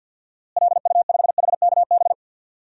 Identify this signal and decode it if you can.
Morse code, heard as a keyed tone, about 51 wpm, 700 Hz. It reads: LU5HKB
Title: DXCC-100 Call Signs - 6 Characters - Repeated 3x - 50wpm